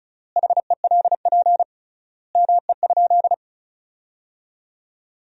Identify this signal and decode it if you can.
Morse code, heard as a keyed tone, about 34 wpm, 700 Hz. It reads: HELP ME?
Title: 3x Word-Spacing - Sentences from Top 200 Words - 35wpm